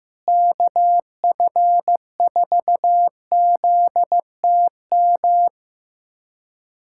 Morse code, heard as a keyed tone, about 15 wpm, 700 Hz. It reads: KF4ZTM
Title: USA Callsign DE Callsign - 6 Characters - 15wpm